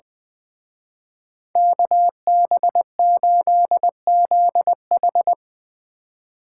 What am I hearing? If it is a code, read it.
Morse code, heard as a keyed tone, about 22 wpm, 700 Hz. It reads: KB8ZH
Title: Speed Racing - USA Call Signs 20wpm